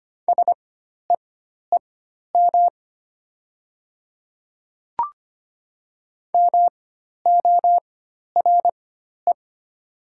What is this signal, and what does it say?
Morse code, heard as a keyed tone, about 25 wpm, spaced as for 12 wpm, 700 Hz. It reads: SEEM MORE
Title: 4x Character Spacing - Top 300 Words - 25wpm